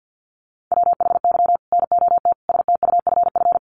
Morse code, heard as a keyed tone, about 50 wpm, 700 Hz. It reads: WHY NOT STUFF
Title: Sentences from Top 1400 Words 50wpm